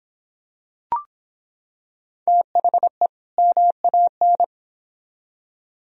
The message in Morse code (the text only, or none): THE MAN